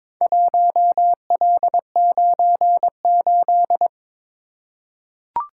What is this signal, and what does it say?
Morse code, heard as a keyed tone, about 22 wpm, 700 Hz. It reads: 1L98
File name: practice / 22wpm / Four Letter-Number Combinations - 22wpm